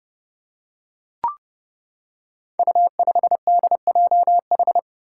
Morse code, heard as a keyed tone, about 28 wpm, 700 Hz. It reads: U5DJH
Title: Five Challenging Letter-Number Combinations 30wpm